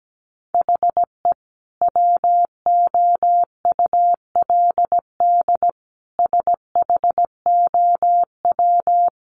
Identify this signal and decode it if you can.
Morse code, heard as a keyed tone, about 17 wpm, 700 Hz. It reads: HE WOULD SHOW
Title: Sentences from Top 1300 Words 17wpm